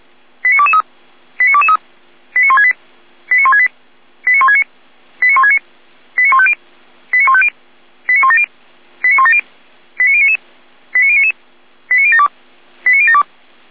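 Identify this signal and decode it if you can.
Five-tone selective call (ZVEI), heard as quick runs of five tones, tones of about 80 ms. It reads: 89223 89223 89168 89168 89169 89169 89150 89150 89170 89170 89099 89099 89072 89072
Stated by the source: Received by SDR as ZVEI five-tone selective call 89167.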